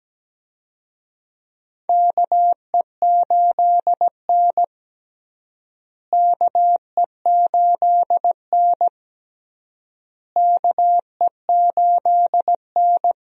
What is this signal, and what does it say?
Morse code, heard as a keyed tone, about 17 wpm, 700 Hz. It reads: KE8N KE8N KE8N
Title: USA Call Signs - 4 Characters - Repeated 3x - 17wpm